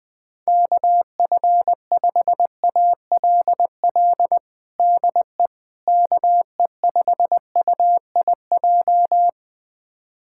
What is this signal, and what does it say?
Morse code, heard as a keyed tone, about 20 wpm, 700 Hz. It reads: KF5ALL DE KE5UIJ
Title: USA Callsign DE Callsign 20wpm